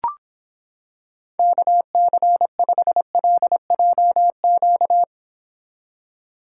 Morse code, heard as a keyed tone, about 26 wpm, 700 Hz. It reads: KC5LJQ